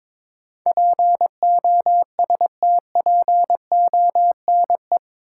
Morse code, heard as a keyed tone, about 22 wpm, 700 Hz. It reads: POSTPONE
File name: practice / 22wpm / Eight Letter Words - 22wpm